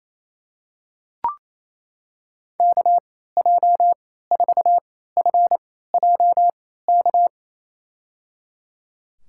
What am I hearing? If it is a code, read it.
Morse code, heard as a keyed tone, about 28 wpm, spaced as for 16 wpm, 700 Hz. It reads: KJ4FJK
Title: USA Call Signs - 6 Characters - 3x Character Spacing - 28wpm